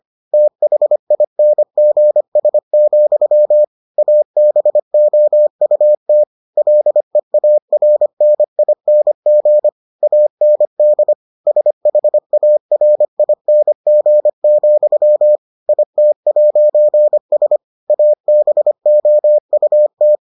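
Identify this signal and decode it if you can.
Morse code, heard as a keyed tone, about 25 wpm, 600 Hz. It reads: THINGS, ABOUT LEARNING AND SHARING, IT'S ABOUT
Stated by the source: Morse edition of the Foundations of Amateur Radio episodes.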